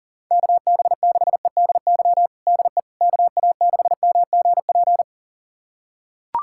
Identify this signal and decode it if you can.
Morse code, heard as a keyed tone, about 40 wpm, 700 Hz. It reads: KB6EDY DE KA6MGP
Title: USA Callsign DE Callsign - 6 Characters - 40wpm